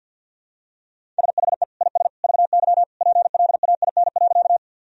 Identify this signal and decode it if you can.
Morse code, heard as a keyed tone, about 50 wpm, 700 Hz. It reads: SHE IS VY PLAIN.
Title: Sentences from Top 400 Words and Abbreviations 50wpm